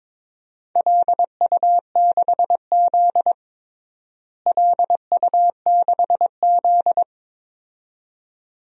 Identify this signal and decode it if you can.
Morse code, heard as a keyed tone, about 22 wpm, 700 Hz. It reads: LU6Z LU6Z
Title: DXCC-100 Call Signs - 4 Characters - Repeated 2x - 22wpm